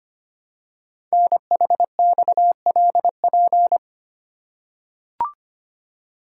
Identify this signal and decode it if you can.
Morse code, heard as a keyed tone, about 25 wpm, 700 Hz. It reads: NHXLP
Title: Code Groups - 5 Letter:Numbers - 25wpm